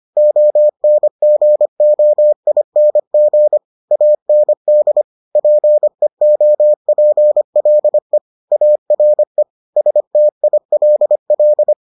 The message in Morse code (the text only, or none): ONGOING AND PEOPLE ARE STILL